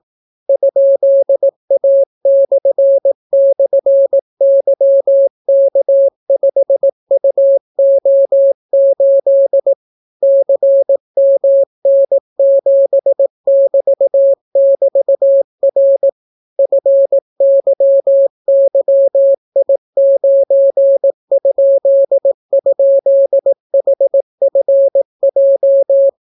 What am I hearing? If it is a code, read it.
Morse code, heard as a keyed tone, about 18 wpm, 550 Hz. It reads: ?A//YK5UO8 CMN7<BT><BT>R FYYI9??HFJ